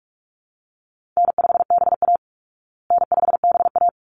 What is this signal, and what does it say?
Morse code, heard as a keyed tone, about 44 wpm, 700 Hz. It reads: N5BA N5BA
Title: Call Signs Repeated 3x - 45wpm